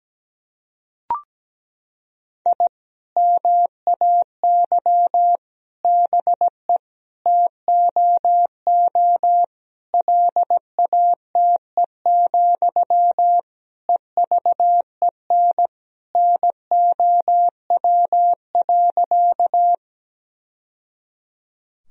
Morse code, heard as a keyed tone, about 17 wpm, 700 Hz. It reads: I MAY BE TOO LATE, EVEN NOW.
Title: Sentences from Top 1000 Words - Review Part 1 - 17wpm